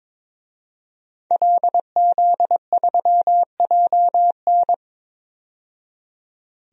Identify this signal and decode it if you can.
Morse code, heard as a keyed tone, about 22 wpm, 700 Hz. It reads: LZ3JN
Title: DXCC-100 Call Signs - 5 Characters - 22wpm